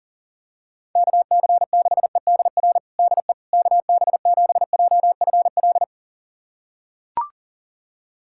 Morse code, heard as a keyed tone, about 40 wpm, 700 Hz. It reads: KC6EDR DE KB7JFL
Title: USA Callsign DE Callsign 40wpm